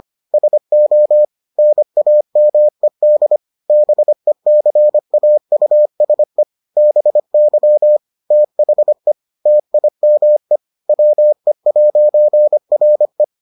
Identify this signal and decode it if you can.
Morse code, heard as a keyed tone, about 25 wpm, 600 Hz. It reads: SO NAMED BECAUSE BY THE TIME WE'RE